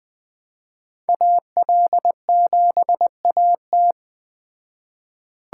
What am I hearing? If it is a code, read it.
Morse code, heard as a keyed tone, about 20 wpm, 700 Hz. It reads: AL7AT
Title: Speed Racing - USA 2x2 Call Signs 17wpm